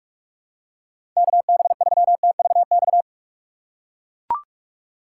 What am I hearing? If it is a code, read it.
Morse code, heard as a keyed tone, about 43 wpm, 700 Hz. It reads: KB3TVX
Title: USA Call Signs - 6 Characters - 4x Character Spacing - 45wpm